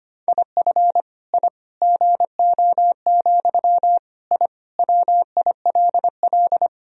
Morse code, heard as a keyed tone, about 25 wpm, 700 Hz. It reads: IF I GO, I WILL
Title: Sentences from Top 1000 Words - Review Part 3 - 25wpm